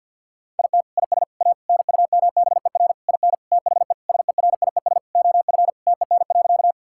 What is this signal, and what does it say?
Morse code, heard as a keyed tone, about 50 wpm, 700 Hz. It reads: IT IS A NUMBER IN THE SERIES OF TEN.
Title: Sentences from Top 1000 Words - Review Part 5 - 50wpm